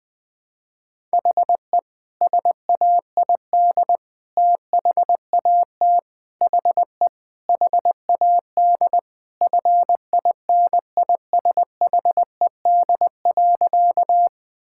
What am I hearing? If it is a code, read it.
Morse code, heard as a keyed tone, about 21 wpm, 700 Hz. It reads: HE SAID THAT HE HAD FINISHED.
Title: Sentences with 3-Word Phrases - Spoken Beginning - 20wpm